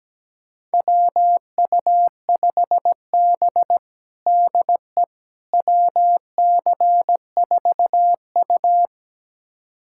Morse code, heard as a keyed tone, about 17 wpm, 700 Hz. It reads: WU5B DE WC4U
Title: USA Callsign DE Callsign - 4 Characters - 17wpm